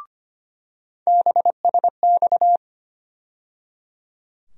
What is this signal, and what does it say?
Morse code, heard as a keyed tone, about 25 wpm, 700 Hz. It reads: BSX
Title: USA Call Sign Postfixes - 3 Letters 25wpm